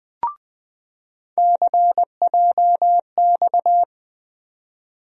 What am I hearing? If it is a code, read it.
Morse code, heard as a keyed tone, about 20 wpm, 700 Hz. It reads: CJX
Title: USA Call Sign Postfixes - 3 Letters 20wpm